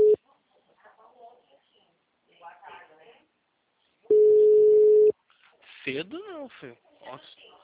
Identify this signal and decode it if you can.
Ringback tone (Europe, 425 Hz), with speech.